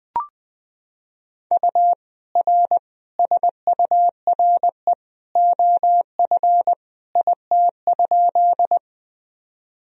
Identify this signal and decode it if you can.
Morse code, heard as a keyed tone, about 20 wpm, 700 Hz. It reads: U R SURE OF IT?
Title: Sentences from Top 300 Words and Abbreviations 20wpm